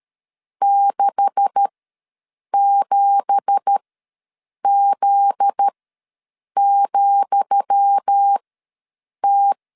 Morse code, heard as a keyed tone, about 13 wpm, spaced as for 7 wpm, 800 Hz. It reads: 67Z,T
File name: LEZIONI / 10 SIGNS / 10D COMA ws 12wpm 3618